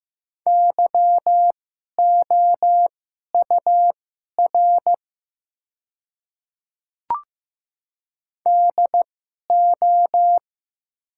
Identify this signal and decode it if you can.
Morse code, heard as a keyed tone, about 15 wpm, spaced as for 11 wpm, 700 Hz. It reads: YOUR DO
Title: 2x Character Spacing - Top 100 Words - 15wpm